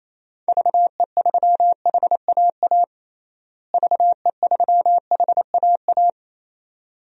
Morse code, heard as a keyed tone, about 28 wpm, 700 Hz. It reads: VE3HAA VE3HAA
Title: DXCC-100 Call Signs - 6 Characters - Repeated 2x - 28wpm